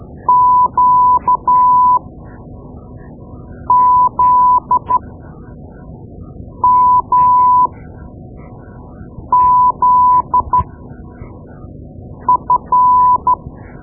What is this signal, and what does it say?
Morse code, heard as a keyed tone, about 13 wpm, spaced as for 4 wpm, 1000 Hz. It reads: QZMZF
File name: Morse Lessons / Morse Lessons / Tape-2B